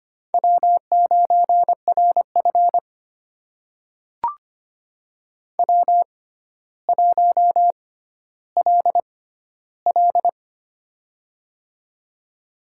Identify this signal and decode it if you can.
Morse code, heard as a keyed tone, about 25 wpm, spaced as for 13 wpm, 700 Hz. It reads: W9RF W1LL
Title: USA Call Signs - 4 Characters - 6x Character Spacing - 25wpm